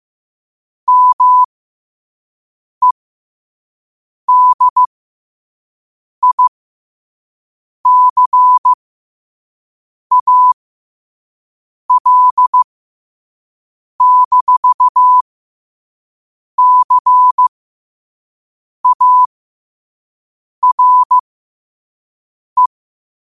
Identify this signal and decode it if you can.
Morse code, heard as a keyed tone, about 15 wpm, spaced as for 5 wpm, 1000 Hz. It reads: MEDICAL-CARE
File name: qotdfiles / 5wpm